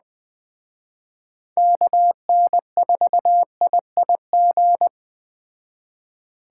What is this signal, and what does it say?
Morse code, heard as a keyed tone, about 21 wpm, 700 Hz. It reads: KN4IIG